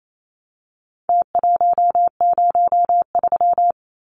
Morse code, heard as a keyed tone, about 28 wpm, 700 Hz. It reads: T103
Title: Numbers - Contest Style - 4-Digit 28wpm